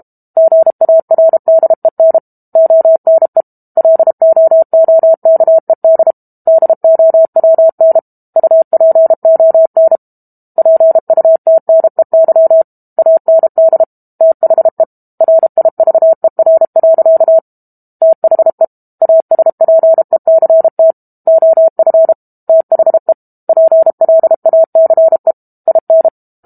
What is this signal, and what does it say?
Morse code, heard as a keyed tone, about 32 wpm, 650 Hz. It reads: GARDEN ONE LOOKED DOWN UPON PUTNEY AND THE RIVER. THE ASPECT OF THE PLACE IN
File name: Book Two / 06-Chapter 6 - 32 WPM